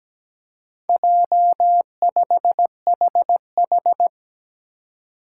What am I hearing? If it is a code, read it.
Morse code, heard as a keyed tone, about 17 wpm, 700 Hz. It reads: J5HH
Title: Four Challenging Letter-Number Combinations 17wpm